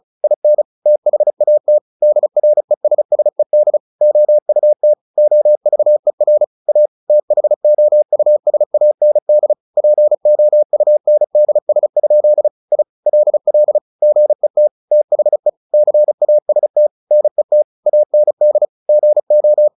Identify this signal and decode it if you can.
Morse code, heard as a keyed tone, about 34 wpm, 600 Hz. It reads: IN THAT DRESSED OUT OVER A THOUSAND POUNDS? I LL GET THE CAST NET AND GO
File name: the old man and the sea / the old man and the sea 1 35wpm